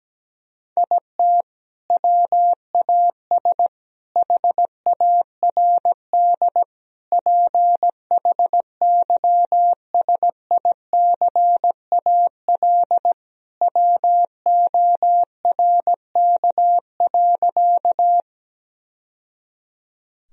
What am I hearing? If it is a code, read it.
Morse code, heard as a keyed tone, about 17 wpm, 700 Hz. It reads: IT WAS HARD PHYSICAL WORK.
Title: Sentences from Top 1000 Words - Review Part 4 - 17wpm